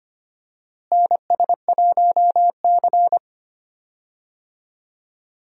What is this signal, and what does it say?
Morse code, heard as a keyed tone, about 25 wpm, 700 Hz. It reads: NS1C
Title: USA Call Signs - 4 Characters - Repeated 3x - 25wpm